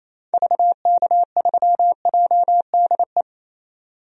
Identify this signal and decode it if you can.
Morse code, heard as a keyed tone, about 28 wpm, 700 Hz. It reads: VK3JDE
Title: DXCC-100 Call Signs - 6 Characters - 28wpm